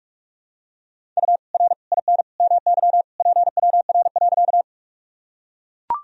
Morse code, heard as a keyed tone, about 44 wpm, 700 Hz. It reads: U R IN MY PWR.
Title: Sentences from Top 500 Words and Abbreviations 45wpm